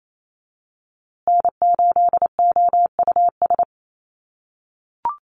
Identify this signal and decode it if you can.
Morse code, heard as a keyed tone, about 28 wpm, 700 Hz. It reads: N8OUS
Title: USA Call Signs Repeated 3x 28wpm